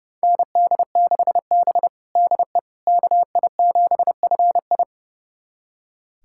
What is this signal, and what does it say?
Morse code, heard as a keyed tone, about 32 wpm, 700 Hz. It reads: ND6B DE KI7FI
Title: USA Callsign DE Callsign 30wpm